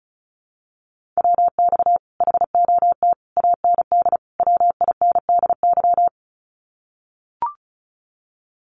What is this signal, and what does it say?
Morse code, heard as a keyed tone, about 36 wpm, 700 Hz. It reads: WX HOT AND WINDY
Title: Common QSO Phrases Encore 35wpm